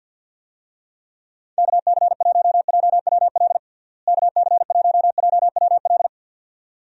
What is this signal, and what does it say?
Morse code, heard as a keyed tone, about 50 wpm, 700 Hz. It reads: KC1JWL KC1JWL
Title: USA Call Signs - 6 Characters - Repeated 2x - 50wpm